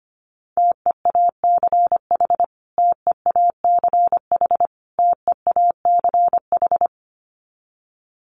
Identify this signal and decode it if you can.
Morse code, heard as a keyed tone, about 25 wpm, 700 Hz. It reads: TEACH TEACH TEACH
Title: Top 500 Words Repeated 25wpm